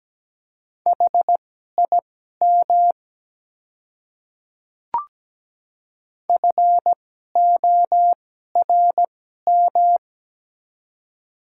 Morse code, heard as a keyed tone, about 17 wpm, spaced as for 12 wpm, 700 Hz. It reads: HIM FORM
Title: 2x Character Spacing - Top 200 Words - 17wpm